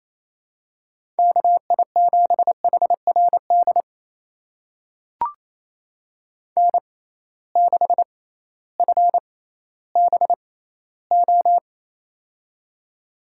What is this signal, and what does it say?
Morse code, heard as a keyed tone, about 28 wpm, spaced as for 15 wpm, 700 Hz. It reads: KI7HRD N6FBO
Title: USA Call Signs - 6x Character Spacing - 28wpm